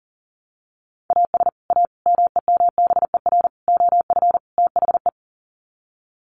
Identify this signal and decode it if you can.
Morse code, heard as a keyed tone, about 40 wpm, 700 Hz. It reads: AS A MEMBER OF THE
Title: Sets of 5 Words 40wpm